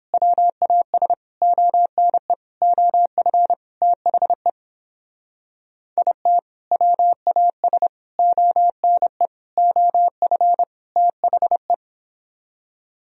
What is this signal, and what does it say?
Morse code, heard as a keyed tone, about 30 wpm, 700 Hz. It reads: WAS ONE OF THE IT WAS ONE OF THE